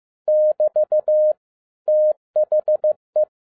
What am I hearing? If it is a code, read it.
Morse code, heard as a keyed tone, about 15 wpm, 610 Hz. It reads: <BT> THE
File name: 2011-10-mp3 / n02-FIELD DAY CONTEST QRV FROM SAKAE VILLAGE 15wpm